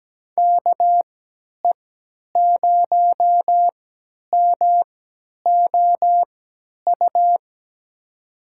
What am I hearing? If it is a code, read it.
Morse code, heard as a keyed tone, about 17 wpm, spaced as for 10 wpm, 700 Hz. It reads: KE0MOU
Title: USA Call Signs - 3x Character Spacing - 17wpm